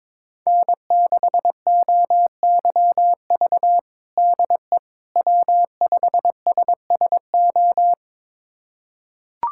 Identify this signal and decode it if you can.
Morse code, heard as a keyed tone, about 22 wpm, 700 Hz. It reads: N6OYV DE W5SSO
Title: USA Callsign DE Callsign - 5 Characters - 22wpm